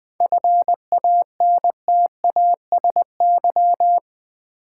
Morse code, heard as a keyed tone, about 20 wpm, 700 Hz. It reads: FANTASY
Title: 7 Letter Words - First 3 Letters Given - 20wpm